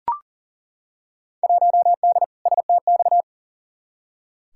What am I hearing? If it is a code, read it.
Morse code, heard as a keyed tone, about 40 wpm, 700 Hz. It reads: 1D STX
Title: Field Day Exchanges - 40wpm